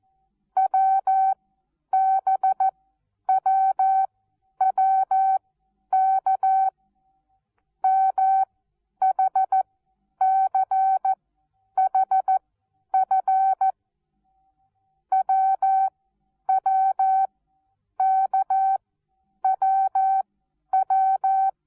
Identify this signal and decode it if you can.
Morse code, heard as a keyed tone, about 14 wpm, spaced as for 9 wpm, 770 Hz. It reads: WBWWK MHCHF WWKWW
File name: Onlinelehrgang / morsen / DJ4UF Morsen19